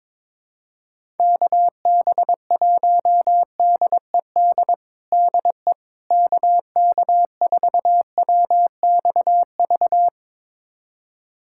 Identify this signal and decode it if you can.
Morse code, heard as a keyed tone, about 21 wpm, 700 Hz. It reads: KB1DED DE KK4WXV
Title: USA Callsign DE Callsign - 6 Characters - 22wpm